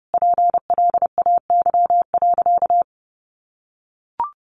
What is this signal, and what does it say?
Morse code, heard as a keyed tone, about 30 wpm, 700 Hz. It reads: PLAY.
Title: Sentences from Top 200 Words 30wpm